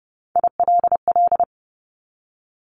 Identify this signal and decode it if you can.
Morse code, heard as a keyed tone, about 30 wpm, 700 Hz. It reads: ILL
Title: Top 1751-2000 Words 30wpm